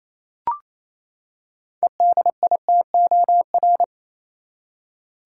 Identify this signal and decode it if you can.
Morse code, heard as a keyed tone, about 28 wpm, 700 Hz. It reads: EDITOR
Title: Six Letter Words - 28wpm